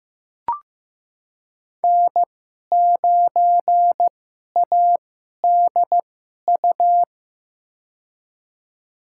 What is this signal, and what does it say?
Morse code, heard as a keyed tone, about 15 wpm, spaced as for 11 wpm, 700 Hz. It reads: N9ADU
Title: USA Call Signs - 5 Characters - 2x Character Spacing - 15wpm